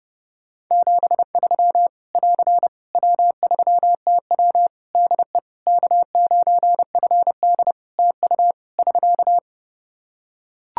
Morse code, heard as a keyed tone, about 30 wpm, 700 Hz. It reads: 73 <AR> W3TW DE K9FD TU <SK>
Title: Practice QSO - Exchange 4 - Ending - 30wpm